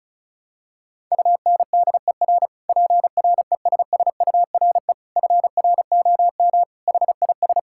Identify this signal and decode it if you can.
Morse code, heard as a keyed tone, about 36 wpm, 700 Hz. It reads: UNDER PRESSURE FROM HIS
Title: Sentences from Top 1000 Words - Review Part 6 - 35wpm